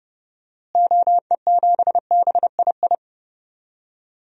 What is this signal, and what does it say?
Morse code, heard as a keyed tone, about 32 wpm, 700 Hz. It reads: OE7BII
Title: DXCC-100 Call Signs - 6 Characters - Repeated 3x - 30wpm